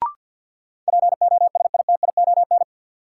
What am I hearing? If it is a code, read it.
Morse code, heard as a keyed tone, about 52 wpm, 700 Hz. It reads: POSITION
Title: Sentences from Top 900 Words - New Word Drill - 50wpm